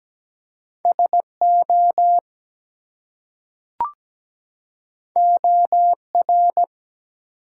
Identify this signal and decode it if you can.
Morse code, heard as a keyed tone, about 17 wpm, 700 Hz. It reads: SO OR
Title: Two Letter Words - 17wpm